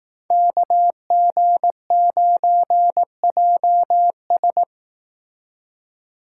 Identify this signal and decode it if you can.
Morse code, heard as a keyed tone, about 18 wpm, 700 Hz. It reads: KG9JS
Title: Speed Racing - USA Call Signs 15wpm